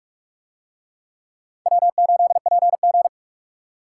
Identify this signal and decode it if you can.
Morse code, heard as a keyed tone, about 49 wpm, 700 Hz. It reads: W8PG